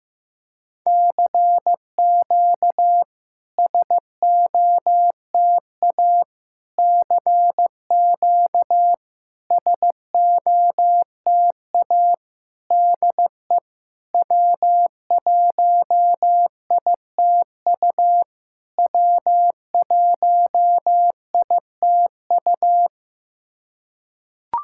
Morse code, heard as a keyed tone, about 15 wpm, 690 Hz. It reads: CQ SOTA CQ SOTA DE W1ITU W1ITU
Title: SOTA QSO Practice 15wpm